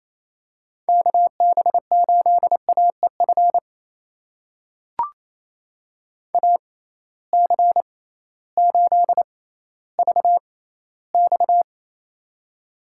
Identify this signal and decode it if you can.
Morse code, heard as a keyed tone, about 28 wpm, spaced as for 15 wpm, 700 Hz. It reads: KB8AEF AC8VX
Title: USA Call Signs - 6x Character Spacing - 28wpm